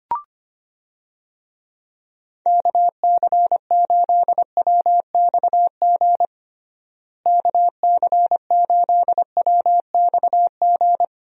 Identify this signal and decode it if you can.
Morse code, heard as a keyed tone, about 25 wpm, 700 Hz. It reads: KC8WXG KC8WXG
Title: USA Call Signs - 6 Characters - Repeated 2x - 25wpm